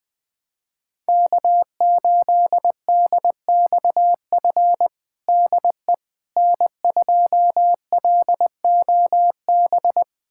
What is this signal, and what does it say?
Morse code, heard as a keyed tone, about 20 wpm, 700 Hz. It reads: K8DXF DE N2LOB
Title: USA Callsign DE Callsign - 5 Characters - 20wpm